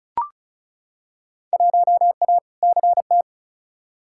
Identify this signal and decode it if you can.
Morse code, heard as a keyed tone, about 36 wpm, 690 Hz. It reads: 1A CT
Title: Field Day Exchanges Encore - 35wpm